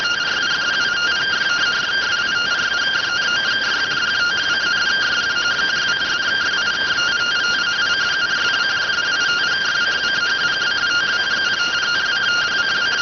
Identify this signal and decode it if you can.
Two-tone teleprinter signal, about 170 Hz shift.